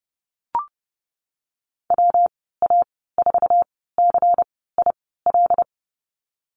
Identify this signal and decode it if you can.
Morse code, heard as a keyed tone, about 30 wpm, spaced as for 17 wpm, 700 Hz. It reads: WA4CIL
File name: Lesson 19g - Letter w - Call Signs 30wpm